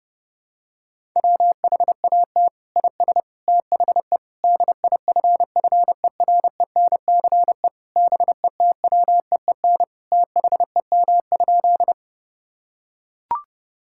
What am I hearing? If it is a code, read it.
Morse code, heard as a keyed tone, about 30 wpm, 700 Hz. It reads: WHAT IS THE DIFFERENCE BETWEEN THEM?
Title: Sentences from Top 1000 Words - Review Part 5 - 30wpm